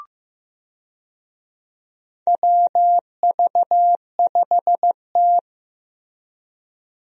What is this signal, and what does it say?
Morse code, heard as a keyed tone, about 16 wpm, 700 Hz. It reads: WV5T
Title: USA Call Signs - 4 Characters - Repeated 3x - 15wpm